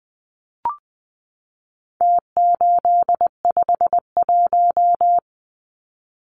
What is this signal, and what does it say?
Morse code, heard as a keyed tone, about 20 wpm, 700 Hz. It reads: T851
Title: Numbers - Contest Style - 4-Digit 20wpm